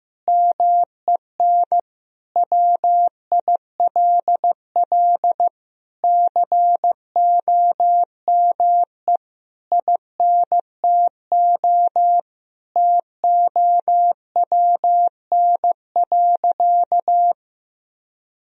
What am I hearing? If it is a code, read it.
Morse code, heard as a keyed tone, about 16 wpm, 700 Hz. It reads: MEN WILL COME INTO TOWN.
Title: Sentences from Top 1000 Words - Review Part 2 - 15wpm